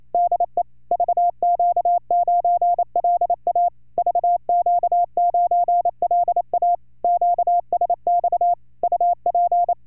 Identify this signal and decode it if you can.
Morse code, heard as a keyed tone, about 28 wpm, 700 Hz. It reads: DE VQ9LA VQ9LA QSX UP